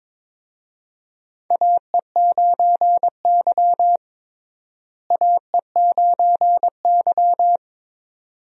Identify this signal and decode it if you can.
Morse code, heard as a keyed tone, about 22 wpm, 700 Hz. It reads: AE9Y AE9Y
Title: USA Call Signs - 4 Characters - Repeated 2x - 22wpm